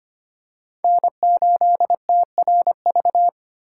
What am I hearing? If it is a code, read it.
Morse code, heard as a keyed tone, about 25 wpm, 700 Hz. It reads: N8TRV
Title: USA Call Signs - 5 Characters - 2x Character Spacing - 25wpm